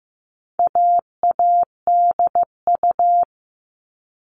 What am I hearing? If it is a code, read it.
Morse code, heard as a keyed tone, about 15 wpm, 700 Hz. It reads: AADU
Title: All Cut Numbers - 4-Digit 15wpm